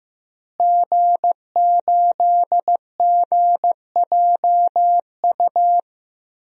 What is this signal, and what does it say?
Morse code, heard as a keyed tone, about 14 wpm, 700 Hz. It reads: G8GJU
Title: DXCC-100 Call Signs - 5 Characters - Repeated 2x - 15wpm